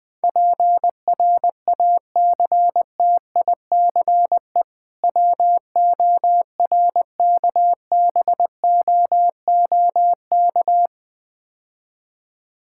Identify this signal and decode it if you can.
Morse code, heard as a keyed tone, about 20 wpm, 700 Hz. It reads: PRACTICE WORKBOOK
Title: Difficult Words - Speed-Racing with Hints - 20wpm